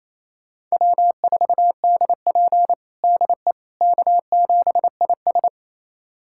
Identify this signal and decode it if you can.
Morse code, heard as a keyed tone, about 27 wpm, 690 Hz. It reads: W4DP DE K7IS
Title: USA Callsign DE Callsign - 4 Characters - 28wpm